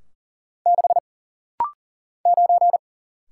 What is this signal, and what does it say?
Morse code, heard as a keyed tone, about 40 wpm, 700 Hz. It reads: 6 9
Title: Single Letter-Number - Rapid-Fire 40wpm